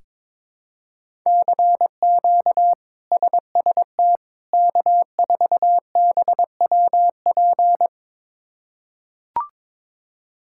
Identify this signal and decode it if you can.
Morse code, heard as a keyed tone, about 22 wpm, 700 Hz. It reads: CQ SST K4BWP
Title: K1USN SST QSO Practice - 22wpm